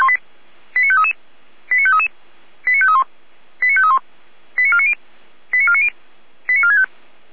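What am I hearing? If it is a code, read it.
Five-tone selective call (ZVEI), heard as quick runs of five tones, tones of about 70 ms. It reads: 89533 89533 89531 89531 89490 89490 89465
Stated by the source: Five-tone ZVEI selective call 89485, received on SDR.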